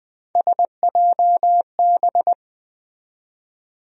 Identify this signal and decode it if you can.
Morse code, heard as a keyed tone, about 20 wpm, 700 Hz. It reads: SJB
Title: USA Call Sign Postfixes - 3 Letters 20wpm